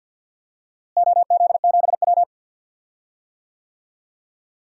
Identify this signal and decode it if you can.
Morse code, heard as a keyed tone, about 52 wpm, 700 Hz. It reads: OZ7W
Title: DXCC-100 Call Signs - 4 Characters - Repeated 2x - 50wpm